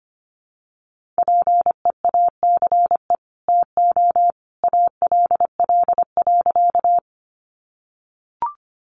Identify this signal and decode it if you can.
Morse code, heard as a keyed tone, about 25 wpm, 700 Hz. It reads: PEACE TO ALL.
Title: Sentences from Top 1500 Words 25wpm